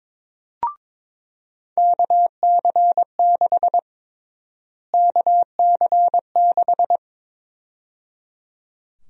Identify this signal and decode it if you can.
Morse code, heard as a keyed tone, about 22 wpm, 700 Hz. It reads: KC6 KC6
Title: USA Call Sign Prefixes - Repeated 2x - 22wpm